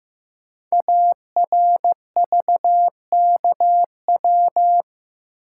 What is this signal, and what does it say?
Morse code, heard as a keyed tone, about 15 wpm, 700 Hz. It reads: ARVKW
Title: Code Groups - 5 Letters - 15wpm